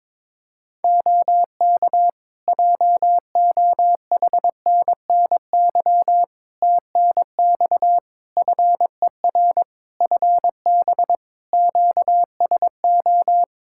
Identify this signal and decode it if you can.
Morse code, heard as a keyed tone, about 22 wpm, 700 Hz. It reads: OK JOHNNY TNX FER FB QSO
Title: Practice QSO - Exchange 4 - Ending - 22wpm